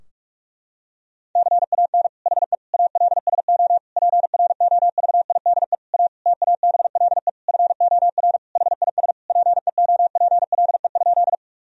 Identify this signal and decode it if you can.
Morse code, heard as a keyed tone, about 46 wpm, 700 Hz. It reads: CAN HE ALSO PROVIDE A TABLE FOR HIS PEOPLE?
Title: Sentences from Top 1000 Words - Review Part 3 - 45wpm